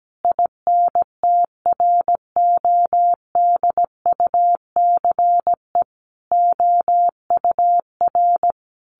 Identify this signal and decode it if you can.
Morse code, heard as a keyed tone, about 17 wpm, 700 Hz. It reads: INTRODUCE OUR
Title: A Sound Of Thunder 17wpm